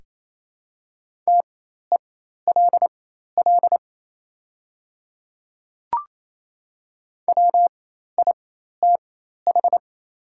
Morse code, heard as a keyed tone, about 28 wpm, spaced as for 13 wpm, 700 Hz. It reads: TELL WITH
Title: 4x Character Spacing - Top 200 Words - 28wpm